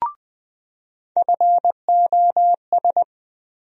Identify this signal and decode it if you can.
Morse code, heard as a keyed tone, about 20 wpm, 700 Hz. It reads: FOS
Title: USA Call Sign Postfixes - 3 Letters 20wpm